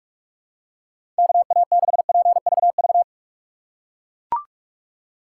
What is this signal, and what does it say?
Morse code, heard as a keyed tone, about 41 wpm, 700 Hz. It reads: KA6PVV
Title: USA Call Signs - 3x Character Spacing - 45wpm